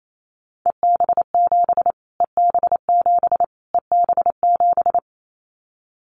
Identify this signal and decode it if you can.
Morse code, heard as a keyed tone, about 28 wpm, 700 Hz. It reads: EB7 EB7 EB7
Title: Callsign Prefixes for Top 25 DXCC - Repeated 3x - 28wpm